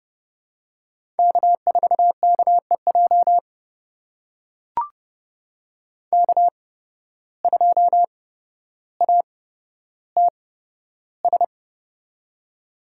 Morse code, heard as a keyed tone, about 29 wpm, spaced as for 13 wpm, 700 Hz. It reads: K4KEJ K2ATS
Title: USA Call Signs - 5 Characters - 8x Character Spacing - 30wpm